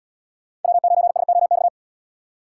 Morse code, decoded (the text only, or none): W9SOX